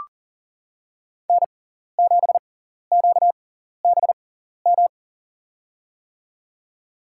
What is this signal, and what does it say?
Morse code, heard as a keyed tone, about 39 wpm, spaced as for 14 wpm, 700 Hz. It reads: N7QBM
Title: USA Call Signs - 5 Characters - 6x Character Spacing - 40wpm